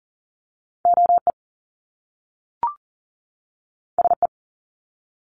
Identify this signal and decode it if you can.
Morse code, heard as a keyed tone, about 40 wpm, 700 Hz. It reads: OE SE
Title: Top Two Letter Combinations in Words 40wpm